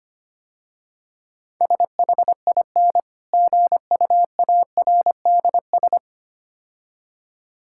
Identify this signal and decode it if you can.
Morse code, heard as a keyed tone, about 25 wpm, 700 Hz. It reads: SHIN GUARDS
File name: practice / 25wpm / Difficult Words - Speed-Racing with Hints - 25wpm